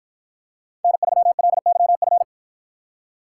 Morse code, heard as a keyed tone, about 53 wpm, 700 Hz. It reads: N3LYF